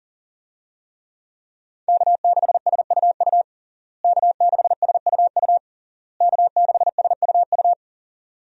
Morse code, heard as a keyed tone, about 40 wpm, 700 Hz. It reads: K6SUU K6SUU K6SUU
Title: USA Call Signs - 5 Characters - Repeated 3x - 40wpm